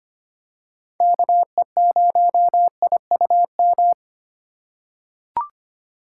Morse code, heard as a keyed tone, about 25 wpm, 700 Hz. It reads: KE0IUM
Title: USA Call Signs - 2x Character Spacing - 25wpm